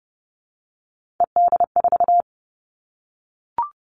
Morse code, heard as a keyed tone, about 30 wpm, 700 Hz. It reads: ED4
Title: All Cut Numbers - 3-Digit 30wpm